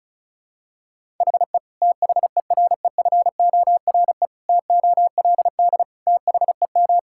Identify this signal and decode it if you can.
Morse code, heard as a keyed tone, about 34 wpm, 700 Hz. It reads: HE THEREFORE TOLD THEM
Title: Sentences from Top 1000 Words - Review Part 5 - 35wpm